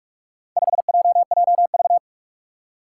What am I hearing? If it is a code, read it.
Morse code, heard as a keyed tone, about 46 wpm, 700 Hz. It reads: 5JJV
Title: Four Challenging Letter-Number Combinations 45wpm